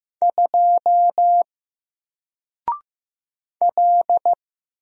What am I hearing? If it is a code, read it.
Morse code, heard as a keyed tone, about 15 wpm, 700 Hz. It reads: 2 L
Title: Single Letter-Number 15wpm